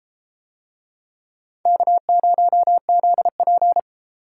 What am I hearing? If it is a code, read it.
Morse code, heard as a keyed tone, about 31 wpm, 700 Hz. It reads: K0ZP